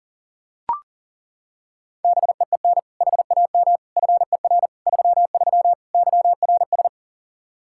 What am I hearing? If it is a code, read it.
Morse code, heard as a keyed tone, about 40 wpm, 700 Hz. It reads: BEEN HAM FER 33 YRS
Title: Practice QSO - Exchange 3 - More Info - 40wpm